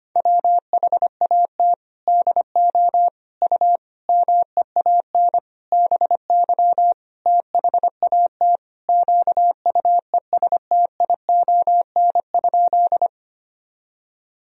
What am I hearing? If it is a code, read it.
Morse code, heard as a keyed tone, about 25 wpm, 700 Hz. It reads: WHAT DO U MEAN BY THAT QUESTION?
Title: Sentences from Top 400 Words and Abbreviations 25wpm